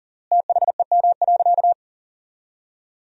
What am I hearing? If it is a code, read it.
Morse code, heard as a keyed tone, about 40 wpm, 700 Hz. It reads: THEM.
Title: 3x Word-Spacing - Sentences from Top 100 Words - 40wpm